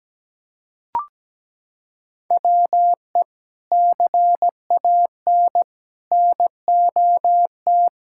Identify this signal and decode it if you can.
Morse code, heard as a keyed tone, about 17 wpm, 700 Hz. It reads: WE CAN NOT
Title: Sentences from Top 1000 Words - Review Part 6 - 17wpm